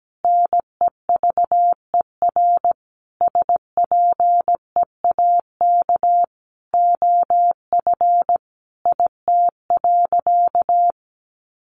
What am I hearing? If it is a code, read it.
Morse code, heard as a keyed tone, about 17 wpm, 700 Hz. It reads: NEVER SPEAK OF IT.
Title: Sentences from Top 1500 Words 17wpm